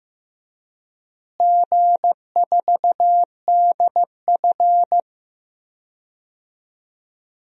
Morse code, heard as a keyed tone, about 16 wpm, 700 Hz. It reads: G4DF
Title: DXCC-100 Call Signs - 4 Characters - Repeated 2x - 15wpm